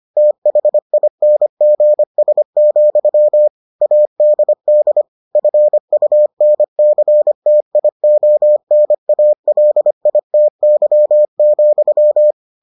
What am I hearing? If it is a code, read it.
Morse code, heard as a keyed tone, about 25 wpm, 600 Hz. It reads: THINGS, ADD FUNCTIONALITY,